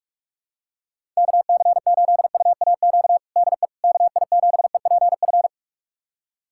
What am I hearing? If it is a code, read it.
Morse code, heard as a keyed tone, about 46 wpm, 700 Hz. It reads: KC8UAQ DE KI7EPF
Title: USA Callsign DE Callsign - 6 Characters - 45wpm